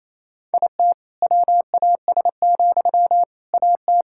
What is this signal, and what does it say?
Morse code, heard as a keyed tone, about 28 wpm, 700 Hz. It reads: IT WAS, AT
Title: Sentences from Top 1000 Words - Review Part 6 - 28wpm